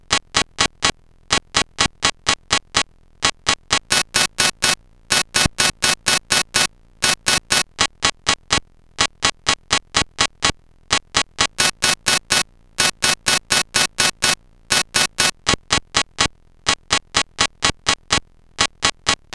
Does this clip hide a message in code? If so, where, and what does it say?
No.